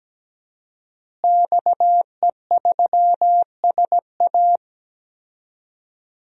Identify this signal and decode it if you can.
Morse code, heard as a keyed tone, about 17 wpm, 700 Hz. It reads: XE3SA
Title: DXCC-100 Call Signs - 5 Characters - Repeated 2x - 17wpm